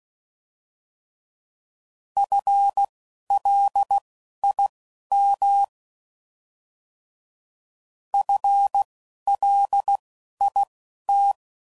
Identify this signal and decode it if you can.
Morse code, heard as a keyed tone, about 16 wpm, spaced as for 11 wpm, 800 Hz. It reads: FLIM FLIT